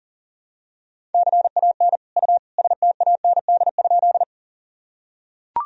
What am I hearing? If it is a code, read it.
Morse code, heard as a keyed tone, about 40 wpm, 700 Hz. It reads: CAN U STAND?
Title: Sentences from Top 300 Words and Abbreviations 40wpm